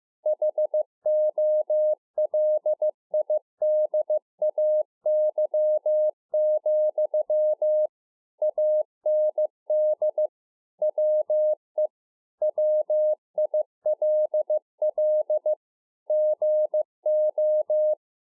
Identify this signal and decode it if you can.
Morse code, heard as a keyed tone, about 15 wpm, 610 Hz. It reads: HOLIDAY, AND WE WILL GO